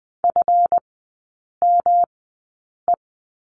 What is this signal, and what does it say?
Morse code, heard as a keyed tone, about 20 wpm, spaced as for 8 wpm, 690 Hz. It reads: FME
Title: 191 Lesson 31e - Letter v - Three Characters 20wpm